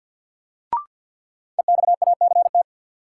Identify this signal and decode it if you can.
Morse code, heard as a keyed tone, about 52 wpm, 700 Hz. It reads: EXACT
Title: Sentences from Top 1300 Words - New Word Drill Alt - 50wpm